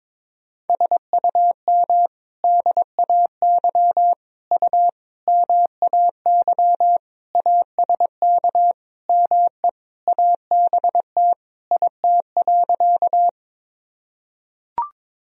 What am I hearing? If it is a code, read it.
Morse code, heard as a keyed tone, about 22 wpm, 700 Hz. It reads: SUM DAY U MAY ASK ME ABT IT.
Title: Sentences from Top 200 Words and Abbreviations 22wpm